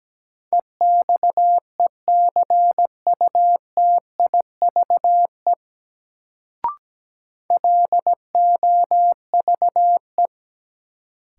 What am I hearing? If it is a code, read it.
Morse code, heard as a keyed tone, about 17 wpm, 700 Hz. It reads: EXECUTIVE LOVE
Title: Sets of 5 Words Encore - New Word Drill - 17wpm